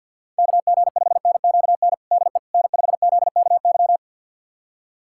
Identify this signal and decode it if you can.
Morse code, heard as a keyed tone, about 52 wpm, 700 Hz. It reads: KG5NQN DE N5ZKY
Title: USA Callsign DE Callsign 50wpm